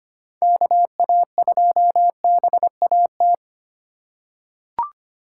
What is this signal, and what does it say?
Morse code, heard as a keyed tone, about 25 wpm, 700 Hz. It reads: KA2BAT
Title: USA Call Signs - 6 Characters - 4x Character Spacing - 25wpm